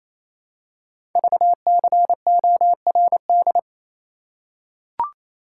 Morse code, heard as a keyed tone, about 27 wpm, 700 Hz. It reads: VCORD
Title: Code Groups - 5 Letters - 28wpm